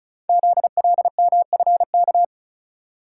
Morse code, heard as a keyed tone, about 36 wpm, 700 Hz. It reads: ZLMFK
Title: Code Groups - 5 Letters - 35wpm